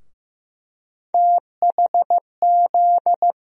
Morse code, heard as a keyed tone, about 15 wpm, 700 Hz. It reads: THZ
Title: USA Call Sign Postfixes - 3 Letters 15wpm